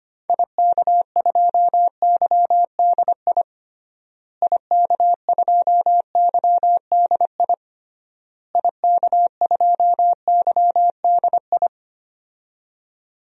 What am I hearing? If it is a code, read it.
Morse code, heard as a keyed tone, about 25 wpm, 700 Hz. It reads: IK2YDI IK2YDI IK2YDI
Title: DXCC-100 Call Signs - 6 Characters - Repeated 3x - 25wpm